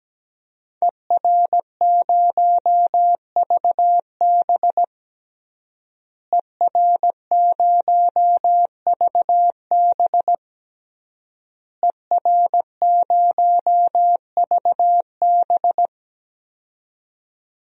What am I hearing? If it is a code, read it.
Morse code, heard as a keyed tone, about 17 wpm, 700 Hz. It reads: ER0VB ER0VB ER0VB
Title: DXCC-100 Call Signs - 5 Characters - Repeated 3x - 17wpm